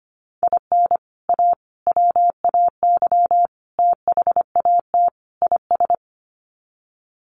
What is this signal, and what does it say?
Morse code, heard as a keyed tone, about 25 wpm, 700 Hz. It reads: IN A WAY THAT IS
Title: Sets of 5 Words Encore 25wpm